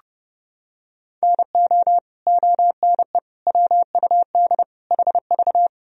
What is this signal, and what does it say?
Morse code, heard as a keyed tone, about 32 wpm, 700 Hz. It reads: NO ONE WUD HV
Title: Sentences from Top 300 Words and Abbreviations 30wpm